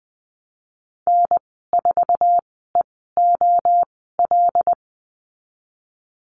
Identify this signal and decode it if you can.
Morse code, heard as a keyed tone, about 20 wpm, spaced as for 14 wpm, 700 Hz. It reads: N4EOL
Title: 088 Lesson 15g - Number 5 - Call Signs 20wpm